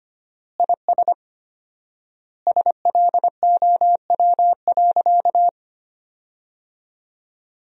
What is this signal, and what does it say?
Morse code, heard as a keyed tone, about 25 wpm, 700 Hz. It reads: IS SLOW.
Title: 4x Word-Spacing - Sentences from Top 400 Words - 25wpm